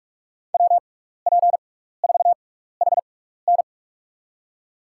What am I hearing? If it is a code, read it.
Morse code, heard as a keyed tone, about 46 wpm, spaced as for 16 wpm, 700 Hz. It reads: WP4HN
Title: USA Call Signs - 5 Characters - 6x Character Spacing - 45wpm